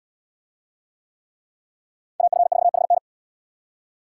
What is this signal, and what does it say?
Morse code, heard as a keyed tone, about 71 wpm, 700 Hz. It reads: AF4HN